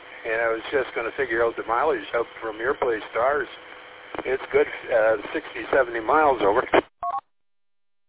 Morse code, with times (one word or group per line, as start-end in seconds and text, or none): none